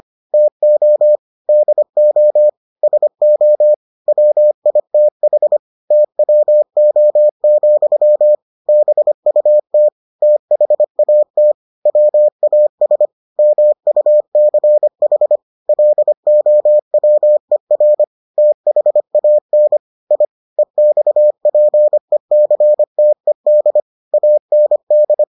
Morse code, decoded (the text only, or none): TO DO SO WITH TWO, BUT THAT WAS MUCH LOWER THAN I EXPECTED AND